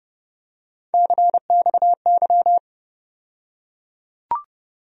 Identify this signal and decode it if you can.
Morse code, heard as a keyed tone, about 30 wpm, 700 Hz. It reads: CXY
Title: USA Call Sign Postfixes - 3 Letters 30wpm